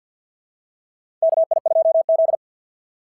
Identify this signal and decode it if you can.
Morse code, heard as a keyed tone, about 50 wpm, 690 Hz. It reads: KI2Z
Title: SOTA QSO Practice 50wpm